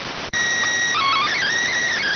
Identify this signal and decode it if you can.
SSTV (slow-scan television), PD 290.